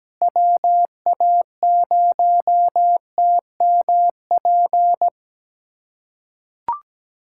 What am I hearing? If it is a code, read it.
Morse code, heard as a keyed tone, about 17 wpm, 700 Hz. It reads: WA0TMP
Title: USA Callsign DE Callsign 17wpm